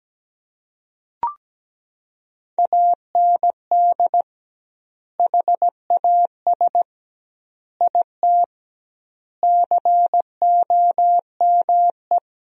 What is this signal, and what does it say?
Morse code, heard as a keyed tone, about 17 wpm, 700 Hz. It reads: AND HAS IT COME
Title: 2x Word-Spacing - Sentences from Top 100 Words - 17wpm